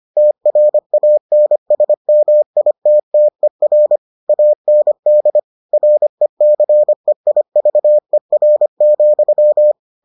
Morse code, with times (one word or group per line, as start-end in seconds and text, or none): 0.2-4.0 TRANSMITTER
4.3-5.4 AND
5.7-9.7 RECEIVER,